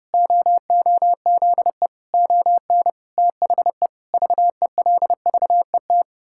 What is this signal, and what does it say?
Morse code, heard as a keyed tone, about 30 wpm, 700 Hz. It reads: OOZE ON THE VELVET
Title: Pangram Challenge v2 - 30wpm